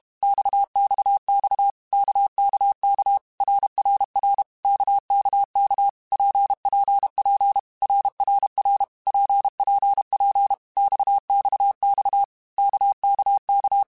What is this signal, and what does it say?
Morse code, heard as a keyed tone, about 32 wpm, 800 Hz. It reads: XXX KKK RRR KKK PPP RRR PPP XXX KKK
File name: Basic Letter Groups By7Groups / RKPX / RKPX@30